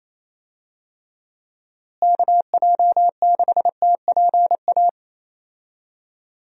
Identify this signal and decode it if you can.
Morse code, heard as a keyed tone, about 28 wpm, 700 Hz. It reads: KJ6TPA
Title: USA Call Signs - 6 Characters - Repeated 2x - 28wpm